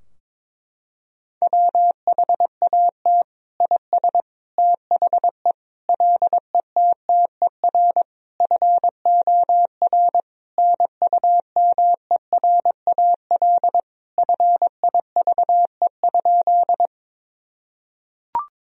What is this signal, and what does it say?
Morse code, heard as a keyed tone, about 22 wpm, 700 Hz. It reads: WHAT IS THE LETTER FOR NUMERAL FIVE?
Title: Sentences from Top 1000 Words - Review Part 2 - 22wpm